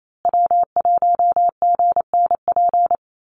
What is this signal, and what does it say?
Morse code, heard as a keyed tone, about 28 wpm, 700 Hz. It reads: W1GNP
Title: USA Call Signs Repeated 3x 28wpm